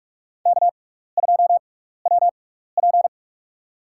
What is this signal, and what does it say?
Morse code, heard as a keyed tone, about 44 wpm, spaced as for 15 wpm, 700 Hz. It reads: K2WP
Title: USA Call Signs - 6x Character Spacing - 45wpm